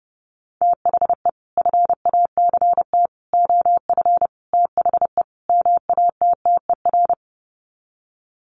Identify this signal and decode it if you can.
Morse code, heard as a keyed tone, about 30 wpm, 700 Hz. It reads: THE FACT OF THE MATTER
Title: Sets of 5 Words 30wpm